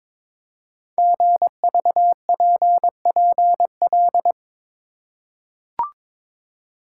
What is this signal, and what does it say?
Morse code, heard as a keyed tone, about 22 wpm, 700 Hz. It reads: GVPPL
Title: Code Groups - 5 Letters - 22wpm